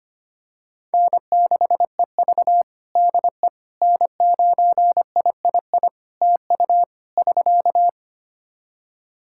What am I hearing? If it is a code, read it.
Morse code, heard as a keyed tone, about 25 wpm, 700 Hz. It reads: N6EV DE N9III TU <SK>
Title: Practice QSO - Exchange 4 - Ending - 25wpm